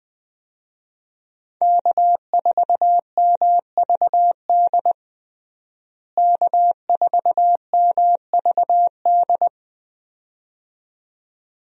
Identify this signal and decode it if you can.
Morse code, heard as a keyed tone, about 20 wpm, 700 Hz. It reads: K4MVD K4MVD
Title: USA Call Signs - 5 Characters - Repeated 2x - 20wpm